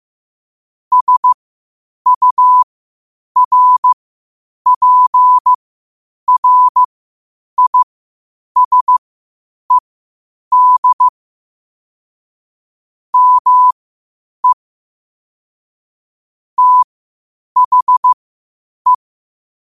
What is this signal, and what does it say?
Morse code, heard as a keyed tone, about 15 wpm, spaced as for 8 wpm, 1000 Hz. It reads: SURPRISED ME THE
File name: qotdfiles / 8wpm